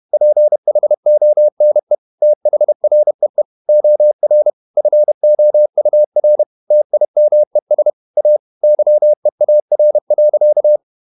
Morse code, heard as a keyed tone, about 31 wpm, 600 Hz. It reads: PHONE THREE OR FOUR TIMES A YEAR.
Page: Never 31wpm | CW med Gnister